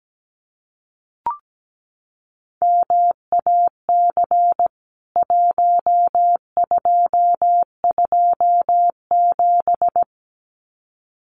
Morse code, heard as a keyed tone, about 17 wpm, 700 Hz. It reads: MAC 1227
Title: CWT Exchange 17wpm